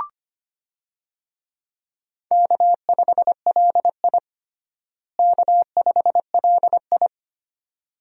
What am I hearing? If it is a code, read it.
Morse code, heard as a keyed tone, about 25 wpm, 700 Hz. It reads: K5LI K5LI
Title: USA Call Signs - 4 Characters - Repeated 2x - 25wpm